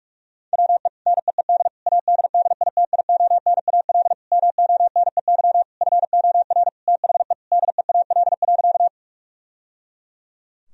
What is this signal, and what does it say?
Morse code, heard as a keyed tone, about 43 wpm, 700 Hz. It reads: WE NEED ADDITIONAL MONEY FOR THE DEAL.
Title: Sentences from Top 1000 Words - Review Part 4 - 45wpm